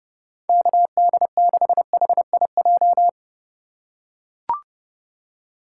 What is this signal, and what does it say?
Morse code, heard as a keyed tone, about 30 wpm, 700 Hz. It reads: KD6HIJ
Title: USA Call Signs - 6 Characters - 4x Character Spacing - 30wpm